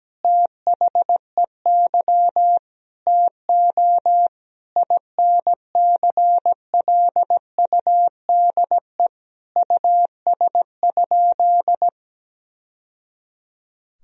Morse code, heard as a keyed tone, about 17 wpm, 700 Hz. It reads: THEY TO INCLUDE US?
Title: Sentences from Top 1000 Words - Review Part 3 - 17wpm